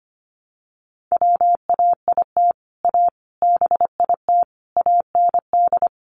Morse code, heard as a keyed tone, about 25 wpm, 700 Hz. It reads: WAIT A BIT AND
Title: Sentences from Top 900 Words 25wpm